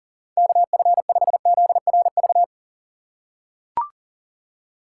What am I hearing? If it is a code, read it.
Morse code, heard as a keyed tone, about 40 wpm, 700 Hz. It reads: KF5ZRV
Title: USA Call Signs - 6 Characters - 3x Character Spacing - 40wpm